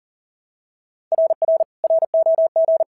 Morse code, heard as a keyed tone, about 40 wpm, 650 Hz. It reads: RR ROG
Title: SOTA QSO Practice 40wpm